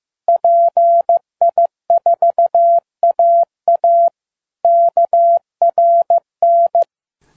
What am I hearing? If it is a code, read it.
Morse code, heard as a keyed tone, about 15 wpm, 680 Hz. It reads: PI4AA KRN
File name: pi4aa / PI4AA Uitzending